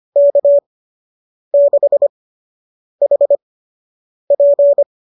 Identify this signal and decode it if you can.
Morse code, heard as a keyed tone, about 25 wpm, spaced as for 8 wpm, 590 Hz. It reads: K6HP
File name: qso003 08